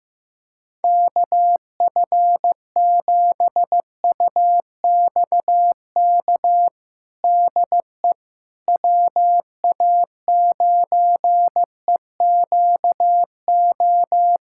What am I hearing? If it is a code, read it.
Morse code, heard as a keyed tone, about 15 wpm, 700 Hz. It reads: KF7UXK DE WA9EQO
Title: USA Callsign DE Callsign 15wpm